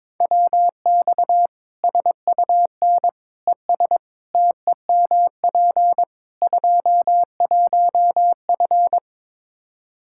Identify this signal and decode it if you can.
Morse code, heard as a keyed tone, about 22 wpm, 700 Hz. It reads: WX SUN ES TEMP 21F
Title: Practice QSO - Exchange 2 - Info - 22wpm